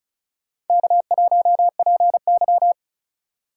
Morse code, heard as a keyed tone, about 35 wpm, 700 Hz. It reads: K1PY
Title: USA Call Signs - 4 Characters - 4x Character Spacing - 35wpm